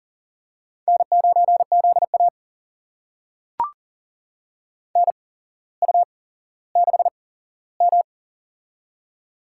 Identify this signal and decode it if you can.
Morse code, heard as a keyed tone, about 40 wpm, spaced as for 17 wpm, 700 Hz. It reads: N9ZA NU6M
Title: USA Call Signs - 4 Characters - 8x Character Spacing - 40wpm